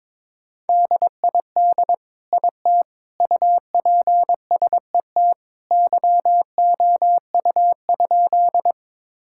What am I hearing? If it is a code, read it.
Morse code, heard as a keyed tone, about 22 wpm, 700 Hz. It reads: DID IT UPSET YOU?
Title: Sentences from Top 1000 Words - Review Part 5 - 22wpm